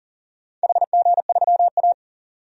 Morse code, heard as a keyed tone, about 40 wpm, 700 Hz. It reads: HG3A
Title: DXCC-100 Call Signs - 4 Characters - 40wpm